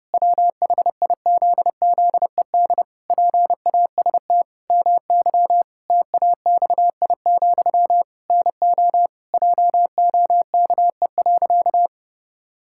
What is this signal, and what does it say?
Morse code, heard as a keyed tone, about 30 wpm, 700 Hz. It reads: WHIZZED PAST MY TAXI, NO JOKE.
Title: Pangram Challenge v2 - 30wpm